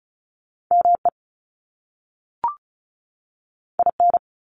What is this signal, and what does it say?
Morse code, heard as a keyed tone, about 35 wpm, 700 Hz. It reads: ME IN
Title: US States 35wpm